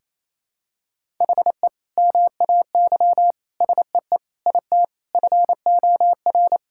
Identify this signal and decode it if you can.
Morse code, heard as a keyed tone, about 28 wpm, 700 Hz. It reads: HE MAY SEE IT FOR
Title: Sentences from Top 1600 Words 28wpm